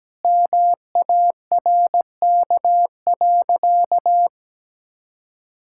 Morse code, heard as a keyed tone, about 17 wpm, 700 Hz. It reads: MARK.
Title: Sentences from Top 1000 Words - Review Part 6 - 17wpm